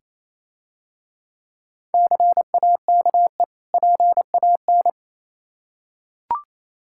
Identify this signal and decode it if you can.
Morse code, heard as a keyed tone, about 27 wpm, 700 Hz. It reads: CAKE PAN
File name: Easy Words - Speed-Racing with Hints - 22wpm